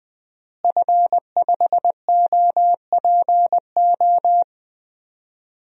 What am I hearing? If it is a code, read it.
Morse code, heard as a keyed tone, about 21 wpm, 700 Hz. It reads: F5OPO
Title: DXCC-100 Call Signs - 5 Characters - Repeated 3x - 20wpm